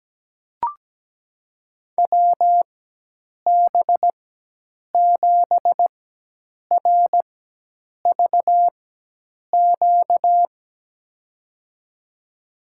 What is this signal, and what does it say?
Morse code, heard as a keyed tone, about 17 wpm, spaced as for 8 wpm, 700 Hz. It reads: WB7RVQ
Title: USA Call Signs - 4x Character Spacing - 17wpm